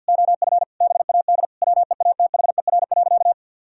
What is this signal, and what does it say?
Morse code, heard as a keyed tone, about 51 wpm, 700 Hz. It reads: OF BAD WEATHER.
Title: Sentences from Top 1000 Words - Review Part 6 - 50wpm